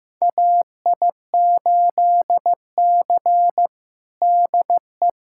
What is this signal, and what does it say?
Morse code, heard as a keyed tone, about 16 wpm, 700 Hz. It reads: AI8C DE
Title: USA Callsign DE Callsign - 4 Characters - 15wpm